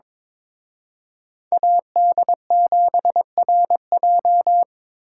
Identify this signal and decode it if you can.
Morse code, heard as a keyed tone, about 26 wpm, 700 Hz. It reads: AD7RJ
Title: Speed Racing - USA 2x2 Call Signs 22wpm